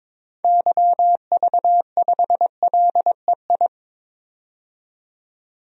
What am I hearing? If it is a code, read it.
Morse code, heard as a keyed tone, about 22 wpm, 700 Hz. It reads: YV5LEI
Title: DXCC-100 Call Signs - 6 Characters - 22wpm